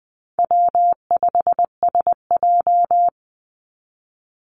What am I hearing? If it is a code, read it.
Morse code, heard as a keyed tone, about 20 wpm, 700 Hz. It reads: W5SJ
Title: Call Signs Repeated 3x - 20wpm